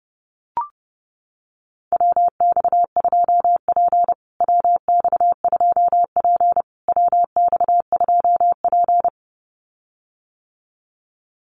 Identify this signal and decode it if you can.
Morse code, heard as a keyed tone, about 30 wpm, 700 Hz. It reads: WX2P WX2P WX2P
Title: USA Call Signs Repeated 3x 30wpm